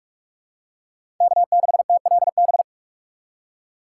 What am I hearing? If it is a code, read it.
Morse code, heard as a keyed tone, about 46 wpm, 700 Hz. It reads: K6TLB
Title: USA Call Signs - 5 Characters - 2x Character Spacing - 45wpm